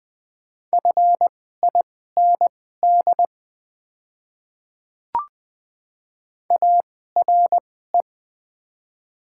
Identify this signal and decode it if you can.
Morse code, heard as a keyed tone, about 20 wpm, spaced as for 15 wpm, 700 Hz. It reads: FIND ARE
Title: 2x Character Spacing - Top 100 Words - 20wpm